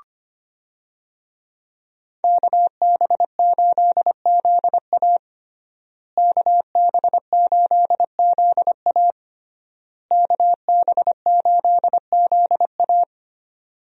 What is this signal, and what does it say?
Morse code, heard as a keyed tone, about 25 wpm, 700 Hz. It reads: KB8ZA KB8ZA KB8ZA
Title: USA Call Signs - 5 Characters - Repeated 3x - 25wpm